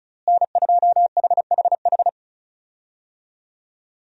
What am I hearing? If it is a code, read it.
Morse code, heard as a keyed tone, about 36 wpm, 700 Hz. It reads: N2HHH
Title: USA Call Signs - 5 Characters - Repeated 2x - 35wpm